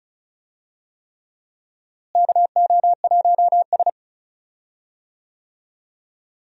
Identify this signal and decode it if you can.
Morse code, heard as a keyed tone, about 36 wpm, 700 Hz. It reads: KO1S
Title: USA Call Signs - 4 Characters - Repeated 3x - 35wpm